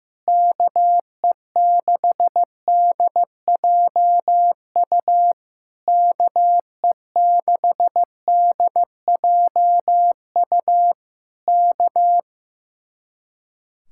Morse code, heard as a keyed tone, about 15 wpm, 700 Hz. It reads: KE6DJU KE6DJU K
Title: Speed-Racing Common QSO Phrases Encore - 15wpm